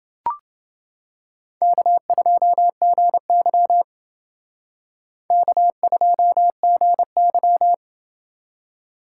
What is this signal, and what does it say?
Morse code, heard as a keyed tone, about 29 wpm, 690 Hz. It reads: K2GY K2GY